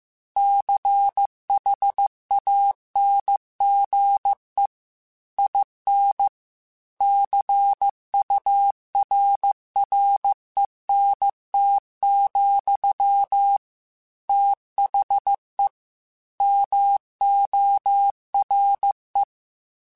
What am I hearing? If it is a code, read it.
Morse code, heard as a keyed tone, about 15 wpm, 800 Hz. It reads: CHANGE IN CURRENT, THE MORE
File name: Technical Subjects / Technical Subjects-mp3s / XL @13